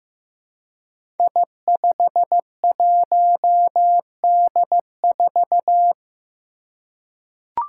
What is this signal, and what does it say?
Morse code, heard as a keyed tone, about 15 wpm, 700 Hz. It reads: I51D4
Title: Code Groups - 5 Letter:Numbers - 15wpm